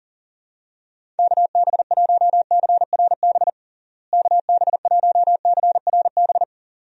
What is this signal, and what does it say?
Morse code, heard as a keyed tone, about 40 wpm, 700 Hz. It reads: KB1CRB KB1CRB
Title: USA Call Signs - 6 Characters - Repeated 2x - 40wpm